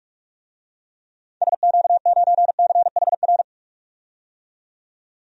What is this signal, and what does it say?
Morse code, heard as a keyed tone, about 46 wpm, 700 Hz. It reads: SQ9CHR
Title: DXCC-100 Call Signs - 6 Characters - 45wpm